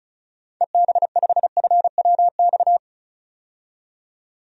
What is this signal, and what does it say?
Morse code, heard as a keyed tone, about 36 wpm, 700 Hz. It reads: EB5FWX
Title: DXCC-100 Call Signs - 6 Characters - 35wpm